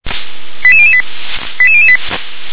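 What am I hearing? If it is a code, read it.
Five-tone selective call (ZVEI), heard as quick runs of five tones, tones of about 70 ms. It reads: 88008 88008
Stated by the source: ZVEI five-tone selective call received on an SDR.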